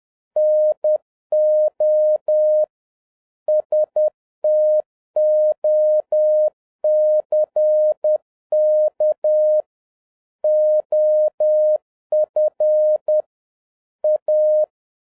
Morse code, heard as a keyed tone, about 10 wpm, 610 Hz. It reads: NO STOCK OF A